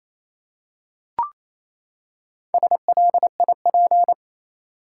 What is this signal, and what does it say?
Morse code, heard as a keyed tone, about 28 wpm, 700 Hz. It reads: SLIP
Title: Four Letter Words - 28wpm